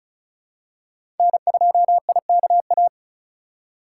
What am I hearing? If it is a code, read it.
Morse code, heard as a keyed tone, about 36 wpm, 700 Hz. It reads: N2IKA
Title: USA Call Signs - 5 Characters - 4x Character Spacing - 35wpm